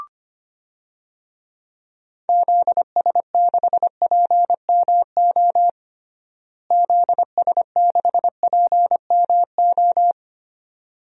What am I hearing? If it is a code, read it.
Morse code, heard as a keyed tone, about 25 wpm, 700 Hz. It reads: ZS6PMO ZS6PMO
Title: DXCC-100 Call Signs - 6 Characters - Repeated 2x - 25wpm